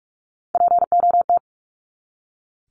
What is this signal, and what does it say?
Morse code, heard as a keyed tone, about 44 wpm, 700 Hz. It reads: POT
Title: Top 751-1000 Words 45wpm